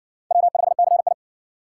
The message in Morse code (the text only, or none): W5CI